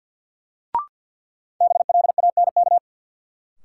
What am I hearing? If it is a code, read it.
Morse code, heard as a keyed tone, about 47 wpm, 700 Hz. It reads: BLANK
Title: Sentences from Top 1500 Words - New Word Drill - 50wpm